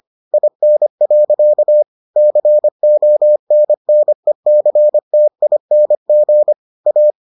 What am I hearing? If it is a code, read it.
Morse code, heard as a keyed tone, about 25 wpm, 600 Hz. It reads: IN. CONNECTING A